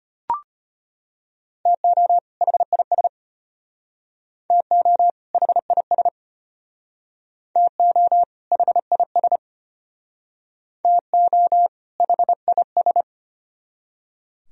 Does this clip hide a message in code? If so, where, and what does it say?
Yes: Morse code from 1.7 to 13.0 s, heard as a keyed tone, about 33 wpm, 700 Hz. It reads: TO HIS TO HIS TO HIS TO HIS